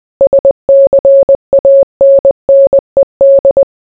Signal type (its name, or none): Morse code